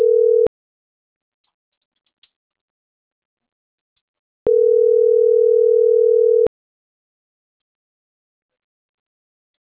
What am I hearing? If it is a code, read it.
Ringback tone (North America), 440 + 480 Hz.